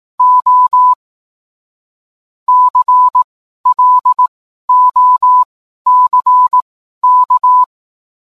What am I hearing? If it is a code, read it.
Morse code, heard as a keyed tone, about 18 wpm, spaced as for 12 wpm, 1000 Hz. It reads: O CLOCK